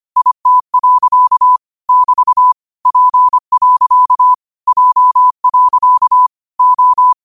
Morse code, heard as a keyed tone, about 25 wpm, 1000 Hz. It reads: IT. <BT> P. J. O
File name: qotdfiles / 25wpm